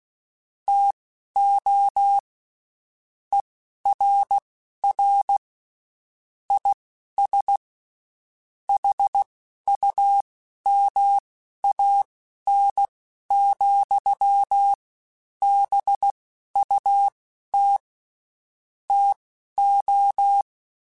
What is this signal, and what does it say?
Morse code, heard as a keyed tone, about 16 wpm, spaced as for 11 wpm, 800 Hz. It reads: TO ERR IS HUMAN, BUT TO
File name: Spaced / Chuckles Spaced-mp3s / HumanErr sp @15